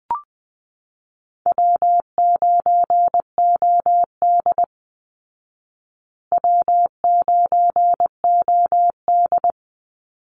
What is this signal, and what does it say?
Morse code, heard as a keyed tone, about 20 wpm, 700 Hz. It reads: W9OD W9OD
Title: Easy Call Signs Repeated 3x - 20wpm